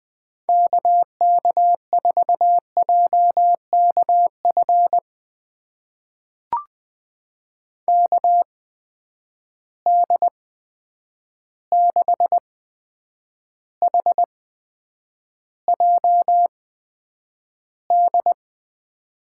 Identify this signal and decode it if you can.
Morse code, heard as a keyed tone, about 20 wpm, spaced as for 9 wpm, 700 Hz. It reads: KK4JKF KD6HJD
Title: USA Call Signs - 6 Characters - 8x Character Spacing - 20wpm